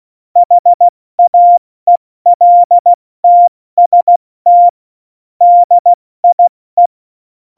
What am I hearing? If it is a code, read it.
Morse code, heard as a keyed tone, about 16 wpm, 700 Hz. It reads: HAELTST DIE
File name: Morsekurs / Witze / Witz 08. Dezember 2020, 70 F80